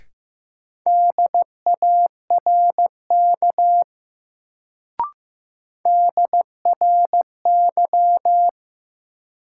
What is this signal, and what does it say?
Morse code, heard as a keyed tone, about 15 wpm, 700 Hz. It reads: DARK DRY
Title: Sentences from Top 500 Words - New Word Drill - 15wpm